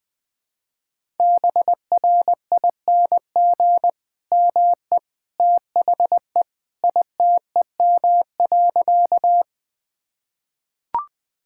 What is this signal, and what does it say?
Morse code, heard as a keyed tone, about 21 wpm, 700 Hz. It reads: BRING ME THE ITEM.
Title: Sentences from Top 1000 Words - Review Part 6 - 20wpm